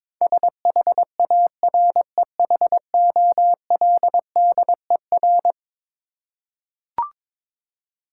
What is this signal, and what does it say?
Morse code, heard as a keyed tone, about 22 wpm, 700 Hz. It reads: SHAREHOLDER
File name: practice / 22wpm / Eleven Letter Words - 22wpm